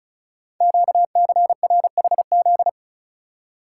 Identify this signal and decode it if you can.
Morse code, heard as a keyed tone, about 36 wpm, 700 Hz. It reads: QCRHZ
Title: Code Groups - 5 Letters - 35wpm